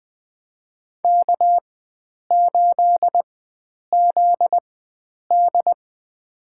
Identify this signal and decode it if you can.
Morse code, heard as a keyed tone, about 20 wpm, spaced as for 9 wpm, 700 Hz. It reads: K8ZD
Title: USA Call Signs - 4x Character Spacing - 20wpm